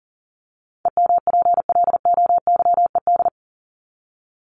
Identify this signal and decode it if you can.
Morse code, heard as a keyed tone, about 40 wpm, 700 Hz. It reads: EMPLOYED
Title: ED Suffix - Top 100 Words with Base Words 40wpm